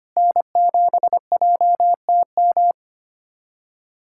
Morse code, heard as a keyed tone, about 25 wpm, 700 Hz. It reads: N7JTM
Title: USA Callsign DE Callsign - 5 Characters - 25wpm